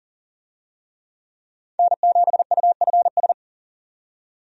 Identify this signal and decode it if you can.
Morse code, heard as a keyed tone, about 40 wpm, 700 Hz. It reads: N7UFS
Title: USA Call Signs - 5 Characters - 40wpm